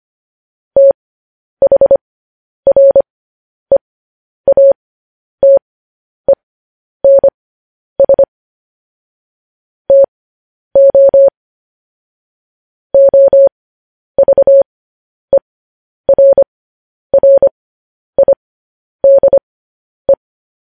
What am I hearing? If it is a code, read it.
Morse code, heard as a keyed tone, about 25 wpm, spaced as for 10 wpm, 550 Hz. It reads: THREATENS TO OVERRIDE
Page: news-headlines-in-morse-code-at-10-wpm